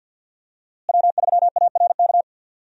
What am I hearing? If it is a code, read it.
Morse code, heard as a keyed tone, about 51 wpm, 700 Hz. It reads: W3ARK
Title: USA Call Signs - 5 Characters - Repeated 2x - 50wpm